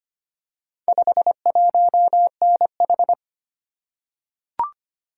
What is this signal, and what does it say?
Morse code, heard as a keyed tone, about 25 wpm, 700 Hz. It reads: 51NH
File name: practice / 25wpm / Four Letter-Number Combinations - 25wpm